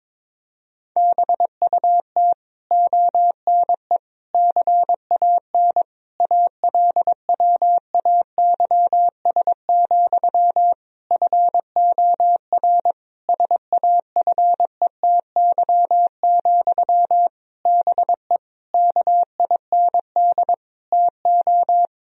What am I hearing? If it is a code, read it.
Morse code, heard as a keyed tone, about 22 wpm, 700 Hz. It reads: BUT ONE CAN ALWAYS, FOR SAFETY, BE KIND TO
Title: Sentences from Top 1000 Words - Review Part 4 - 22wpm